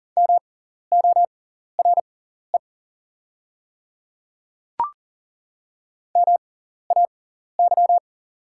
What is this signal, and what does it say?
Morse code, heard as a keyed tone, about 40 wpm, spaced as for 14 wpm, 690 Hz. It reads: MORE MAY
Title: 6x Character Spacing - Top 100 Words - 40wpm